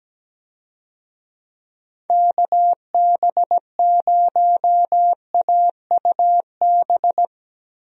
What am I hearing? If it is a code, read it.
Morse code, heard as a keyed tone, about 17 wpm, 700 Hz. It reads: KB0AUB
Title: USA Call Signs - 6 Characters - Repeated 3x - 17wpm